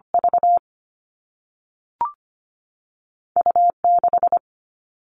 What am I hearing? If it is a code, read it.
Morse code, heard as a keyed tone, about 25 wpm, 700 Hz. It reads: V U6
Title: All Cut Numbers - 2-Digit 25wpm